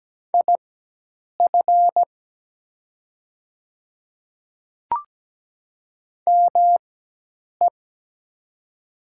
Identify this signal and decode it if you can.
Morse code, heard as a keyed tone, about 17 wpm, spaced as for 8 wpm, 700 Hz. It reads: IF ME
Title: 4x Character Spacing - Top 200 Words - 17wpm